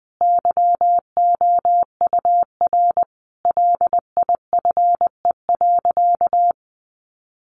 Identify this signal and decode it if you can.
Morse code, heard as a keyed tone, about 19 wpm, 700 Hz. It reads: YOUR LIFE.
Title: Sentences from Top 700 Words 20wpm